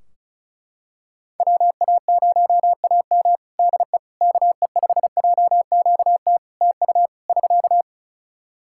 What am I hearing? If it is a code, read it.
Morse code, heard as a keyed tone, about 36 wpm, 700 Hz. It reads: WA0AM DE KE5JQT TU <SK>
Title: Practice QSO - Exchange 4 - Ending - 35wpm